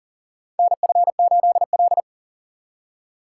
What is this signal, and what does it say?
Morse code, heard as a keyed tone, about 40 wpm, 700 Hz. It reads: NF8L
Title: USA Call Signs - 4 Characters - 6x Character Spacing - 40wpm